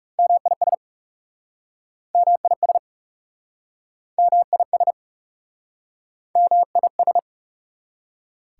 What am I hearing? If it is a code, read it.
Morse code, heard as a keyed tone, about 40 wpm, 700 Hz. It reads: MIS MIS MIS MIS